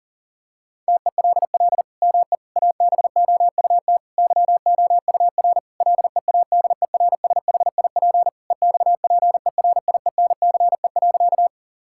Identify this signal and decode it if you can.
Morse code, heard as a keyed tone, about 40 wpm, 700 Hz. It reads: TELL ME ABOUT YOUR LEADERSHIP EXPERIENCE.
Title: Sentences from Top 1600 Words 40wpm